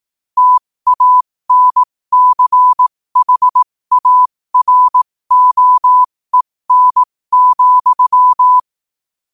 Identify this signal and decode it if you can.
Morse code, heard as a keyed tone, about 18 wpm, spaced as for 15 wpm, 1000 Hz. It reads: TANCHAROEN,
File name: qotdfiles / 15wpm